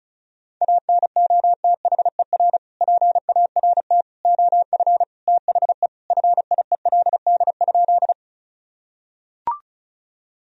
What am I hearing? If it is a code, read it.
Morse code, heard as a keyed tone, about 36 wpm, 700 Hz. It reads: ANOTHER PART OF THE FIELD?
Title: Sentences from Top 1000 Words - Review Part 2 - 35wpm